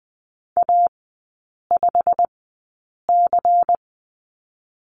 Morse code, heard as a keyed tone, about 20 wpm, spaced as for 8 wpm, 700 Hz. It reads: A5C
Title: 093 Lesson 16e - Letter c - Three Characters 20wpm